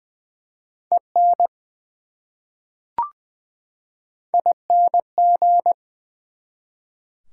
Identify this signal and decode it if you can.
Morse code, heard as a keyed tone, about 20 wpm, 700 Hz. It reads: EN ING
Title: Prefixes and Suffixes 20wpm